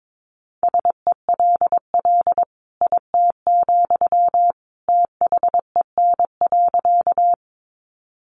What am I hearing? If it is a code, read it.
Morse code, heard as a keyed tone, about 22 wpm, 700 Hz. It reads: SELL IT, THEN.
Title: Sentences from Top 1100 Words 22wpm